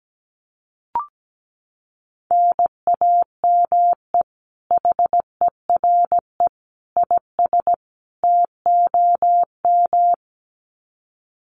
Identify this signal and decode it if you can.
Morse code, heard as a keyed tone, about 17 wpm, 700 Hz. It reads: NAME HERE IS TOM
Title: Common QSO Phrases 17wpm